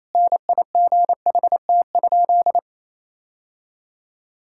Morse code, heard as a keyed tone, about 28 wpm, 700 Hz. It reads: NIGHT?
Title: Sentences from Top 1000 Words - Review Part 5 - 28wpm